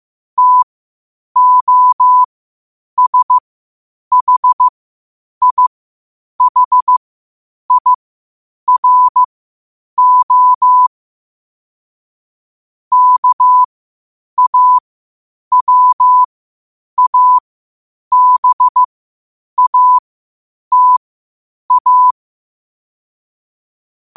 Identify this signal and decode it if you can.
Morse code, heard as a keyed tone, about 15 wpm, spaced as for 8 wpm, 1000 Hz. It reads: TOSHIHIRO KAWABATA
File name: qotdfiles / 8wpm